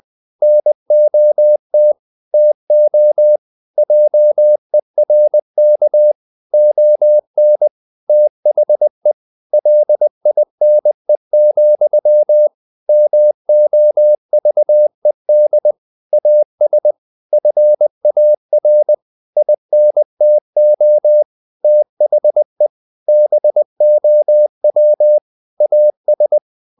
Morse code, heard as a keyed tone, about 20 wpm, 600 Hz. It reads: NOT TO JERK ON THE LINE, MOVED AS FAR INTO THE BOW AS
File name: the old man and the sea / the old man and the sea part 1 20wpm